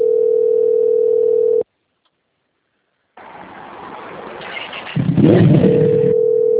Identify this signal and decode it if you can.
Ringback tone (North America), 440 + 480 Hz.
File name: rg-1600-unknown- - - .22804